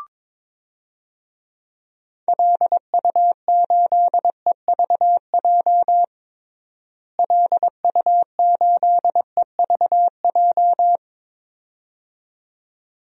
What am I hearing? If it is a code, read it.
Morse code, heard as a keyed tone, about 21 wpm, 700 Hz. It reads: LU8EVJ LU8EVJ
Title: DXCC-100 Call Signs - 6 Characters - Repeated 2x - 22wpm